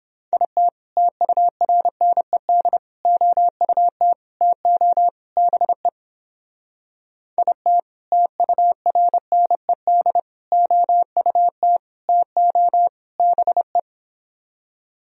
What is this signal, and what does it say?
Morse code, heard as a keyed tone, about 31 wpm, 700 Hz. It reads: IT TURNED OUT TO BE IT TURNED OUT TO BE